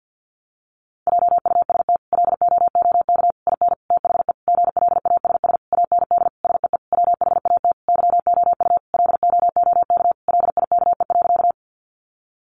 Morse code, heard as a keyed tone, about 50 wpm, 700 Hz. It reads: JUST LOOK IN THE GLASS AND SEE WHAT YOU LOOK LIKE.
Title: Sentences from Top 1300 Words 50wpm